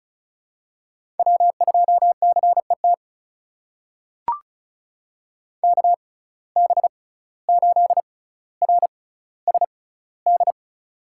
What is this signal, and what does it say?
Morse code, heard as a keyed tone, about 34 wpm, spaced as for 17 wpm, 700 Hz. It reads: W2CET KB8RSD
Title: USA Call Signs - 6x Character Spacing - 35wpm